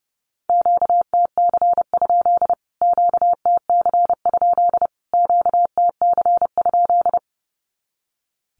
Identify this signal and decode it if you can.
Morse code, heard as a keyed tone, about 30 wpm, 700 Hz. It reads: QTC? QTC? QTC?
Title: Q-Codes Repeated 3x 30wpm